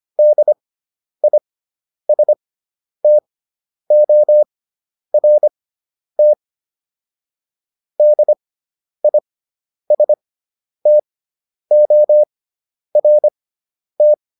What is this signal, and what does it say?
Morse code, heard as a keyed tone, about 25 wpm, spaced as for 10 wpm, 600 Hz. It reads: DISTORT DISTORT